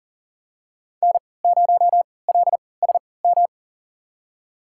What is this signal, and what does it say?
Morse code, heard as a keyed tone, about 40 wpm, spaced as for 23 wpm, 690 Hz. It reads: N0LSM
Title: USA Call Signs - 3x Character Spacing - 40wpm